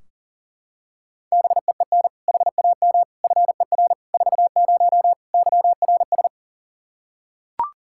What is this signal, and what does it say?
Morse code, heard as a keyed tone, about 37 wpm, 700 Hz. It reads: BEEN HAM FER 40 YRS
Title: Practice QSO - Exchange 3 - More Info - 40wpm